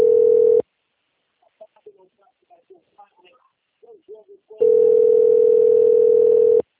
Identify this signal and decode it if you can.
Ringback tone (North America), 440 + 480 Hz, with speech.
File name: out- - - - - .21803